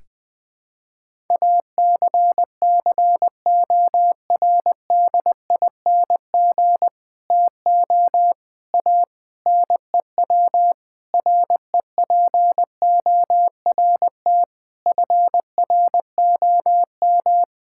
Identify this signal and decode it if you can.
Morse code, heard as a keyed tone, about 21 wpm, 700 Hz. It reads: ACCORDING TO A NEW REPORT FROM
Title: Sentences with 7-Word Phrases - 20wpm